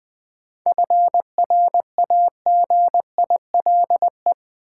Morse code, heard as a keyed tone, about 20 wpm, 700 Hz. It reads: FRAGILE
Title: 7 Letter Words - First 3 Letters Given - 20wpm